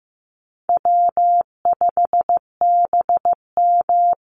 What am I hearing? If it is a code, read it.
Morse code, heard as a keyed tone, about 15 wpm, 700 Hz. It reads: W5BM
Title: Easy Call Signs - 15wpm